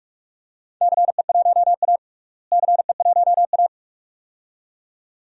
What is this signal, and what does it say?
Morse code, heard as a keyed tone, about 46 wpm, 700 Hz. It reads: CE1A CE1A
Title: DXCC-100 Call Signs - 4 Characters - Repeated 2x - 45wpm